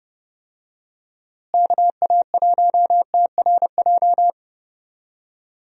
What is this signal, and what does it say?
Morse code, heard as a keyed tone, about 30 wpm, 700 Hz. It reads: KA1TRJ
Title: USA Call Signs - 6 Characters - Repeated 2x - 30wpm